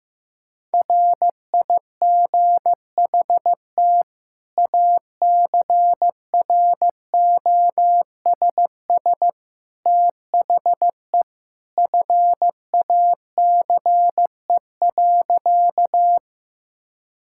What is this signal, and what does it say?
Morse code, heard as a keyed tone, about 15 wpm, 700 Hz. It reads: RIGHT ACROSS THE FACE.
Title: Sentences from Top 1000 Words - Review Part 5 - 15wpm